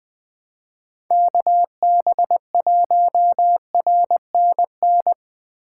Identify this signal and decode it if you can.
Morse code, heard as a keyed tone, about 20 wpm, 700 Hz. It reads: KB1RNN
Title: Speed Racing - USA Call Signs 17wpm